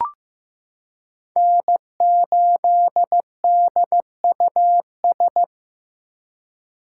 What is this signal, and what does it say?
Morse code, heard as a keyed tone, about 15 wpm, 700 Hz. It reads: N8DUS
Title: K1USN SST QSO Practice - 15wpm